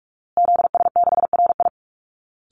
Morse code, heard as a keyed tone, about 44 wpm, 700 Hz. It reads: ZS6RI
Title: Call Signs Repeated 3x - 45wpm